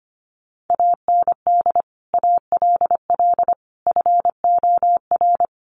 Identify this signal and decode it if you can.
Morse code, heard as a keyed tone, about 25 wpm, 700 Hz. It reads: AND ALL FOR
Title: Sentences from Top 100 Words 25wpm